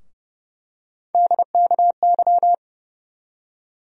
Morse code, heard as a keyed tone, about 30 wpm, 700 Hz. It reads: DKY
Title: USA Call Sign Postfixes - 3 Letters 30wpm